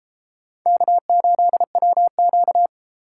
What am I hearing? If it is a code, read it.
Morse code, heard as a keyed tone, about 33 wpm, 700 Hz. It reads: K8WQ